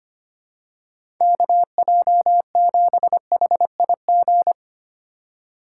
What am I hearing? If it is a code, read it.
Morse code, heard as a keyed tone, about 25 wpm, 700 Hz. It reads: KJ7HIG
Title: USA Call Signs - 6 Characters - 4x Character Spacing - 25wpm